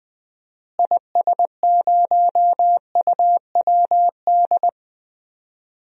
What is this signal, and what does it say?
Morse code, heard as a keyed tone, about 20 wpm, 700 Hz. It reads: IS0UWD
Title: DXCC-100 Call Signs - 6 Characters - Repeated 2x - 20wpm